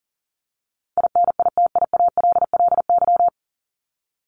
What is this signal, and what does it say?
Morse code, heard as a keyed tone, about 40 wpm, 700 Hz. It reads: INITIALLY
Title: Top 1251-1500 Words 40wpm